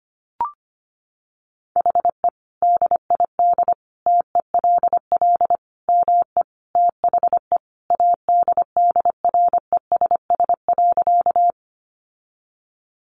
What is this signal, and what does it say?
Morse code, heard as a keyed tone, about 25 wpm, 700 Hz. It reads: HE DID TELL ME THE ADDRESS.
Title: Sentences from Top 1000 Words 25wpm